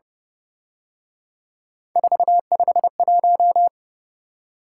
Morse code, heard as a keyed tone, about 29 wpm, 700 Hz. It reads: 451
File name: Speed-Racing - Three Letter-Number Combinations - 25wpm